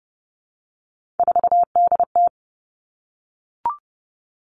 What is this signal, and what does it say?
Morse code, heard as a keyed tone, about 30 wpm, 700 Hz. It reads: 4DT
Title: Three Challenging Letter-Number Combinations 30wpm